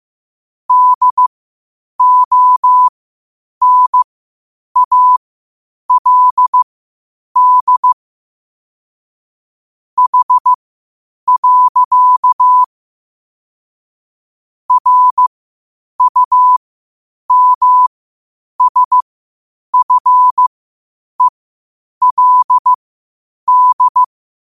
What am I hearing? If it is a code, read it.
Morse code, heard as a keyed tone, about 15 wpm, spaced as for 8 wpm, 1000 Hz. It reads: DONALD H. RUMSFELD
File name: qotdfiles / 8wpm